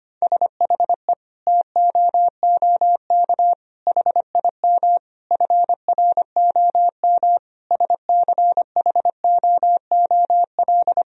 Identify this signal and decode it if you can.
Morse code, heard as a keyed tone, about 25 wpm, 700 Hz. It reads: SHE TOOK HIM FROM SCHOOL
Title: Sentences from Top 1000 Words - Review Part 4 - 25wpm